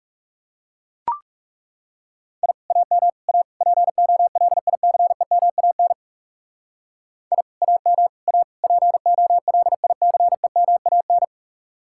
Morse code, heard as a keyed tone, about 38 wpm, 700 Hz. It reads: I AM A POLICEMAN I AM A POLICEMAN